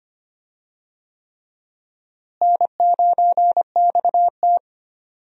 Morse code, heard as a keyed tone, about 25 wpm, 700 Hz. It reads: N9XT
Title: USA Call Signs - 4 Characters - Repeated 3x - 25wpm